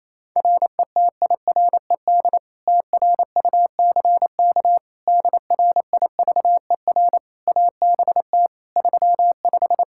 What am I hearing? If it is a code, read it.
Morse code, heard as a keyed tone, about 28 wpm, 700 Hz. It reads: RETIRED TRUCK DRIVER ABT 35
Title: Practice QSO - Exchange 3 - More Info - 28wpm